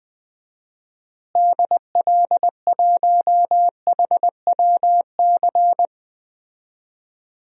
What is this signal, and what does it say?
Morse code, heard as a keyed tone, about 20 wpm, 700 Hz. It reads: DL1HWC
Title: DXCC-100 Call Signs - 6 Characters - Repeated 3x - 20wpm